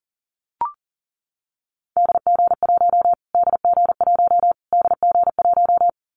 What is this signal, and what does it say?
Morse code, heard as a keyed tone, about 37 wpm, 700 Hz. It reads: DG1 DG1 DG1
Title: Callsign Prefixes for Top 25 DXCC - Repeated 3x - 40wpm